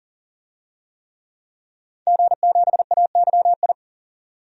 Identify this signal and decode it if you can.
Morse code, heard as a keyed tone, about 40 wpm, 700 Hz. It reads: G7AYI
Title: DXCC-100 Call Signs - 5 Characters - 40wpm